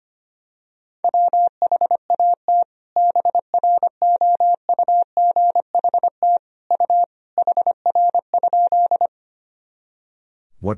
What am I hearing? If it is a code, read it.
Morse code, heard as a keyed tone, about 25 wpm, 700 Hz. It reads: WHAT BROUGHT U HR?
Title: Sentences from Top 500 Words and Abbreviations 25wpm